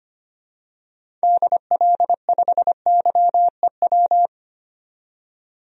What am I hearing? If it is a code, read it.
Morse code, heard as a keyed tone, about 25 wpm, 700 Hz. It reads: DL5YEW
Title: DXCC-100 Call Signs - 6 Characters - 25wpm